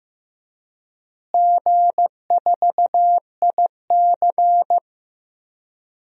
Morse code, heard as a keyed tone, about 15 wpm, 700 Hz. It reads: G4IC
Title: DXCC-100 Call Signs - 4 Characters - Repeated 2x - 15wpm